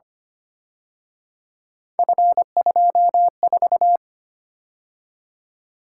Morse code, heard as a keyed tone, about 29 wpm, 700 Hz. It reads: F24
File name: Speed-Racing - Three Letter-Number Combinations - 25wpm